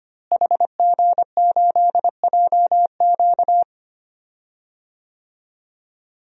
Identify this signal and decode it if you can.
Morse code, heard as a keyed tone, about 25 wpm, 700 Hz. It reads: HG8JQ
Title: DXCC-100 Call Signs - 5 Characters - Repeated 3x - 25wpm